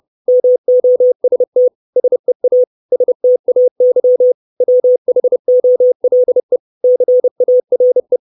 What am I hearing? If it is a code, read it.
Morse code, heard as a keyed tone, about 30 wpm, 500 Hz. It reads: MOST SEA STAY WHOLE CARE
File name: 632 Words 30WPM